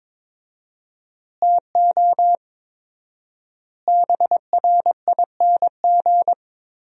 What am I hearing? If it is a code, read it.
Morse code, heard as a keyed tone, about 22 wpm, 700 Hz. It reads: TO BRING
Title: 4x Word-Spacing - Sentences from Top 500 Words - 22wpm